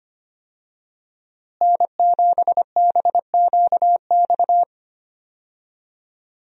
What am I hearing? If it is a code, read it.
Morse code, heard as a keyed tone, about 25 wpm, 700 Hz. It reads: N7BQX
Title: USA Call Signs - 5 Characters - Repeated 3x - 25wpm